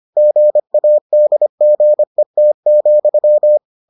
Morse code, heard as a keyed tone, about 25 wpm, 600 Hz. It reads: GADGET,